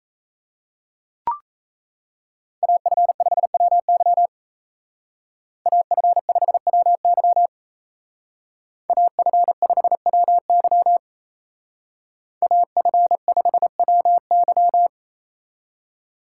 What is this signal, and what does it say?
Morse code, heard as a keyed tone, about 36 wpm, 700 Hz. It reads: AF5WY AF5WY AF5WY AF5WY